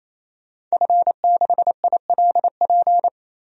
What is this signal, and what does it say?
Morse code, heard as a keyed tone, about 28 wpm, 700 Hz. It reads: F6ILP
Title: DXCC-100 Call Signs - 5 Characters - 28wpm